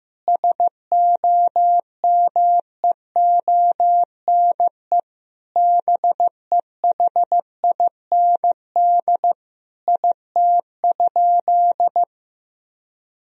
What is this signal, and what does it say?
Morse code, heard as a keyed tone, about 15 wpm, 700 Hz. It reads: SOMEONE BEHIND IT?
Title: Sentences from Top 1000 Words - Review Part 4 - 15wpm